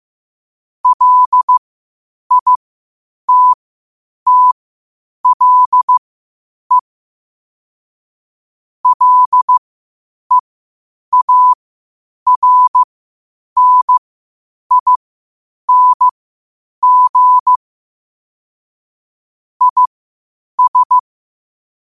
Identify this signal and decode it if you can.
Morse code, heard as a keyed tone, about 15 wpm, spaced as for 8 wpm, 1000 Hz. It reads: LITTLE LEARNING IS
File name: qotdfiles / 8wpm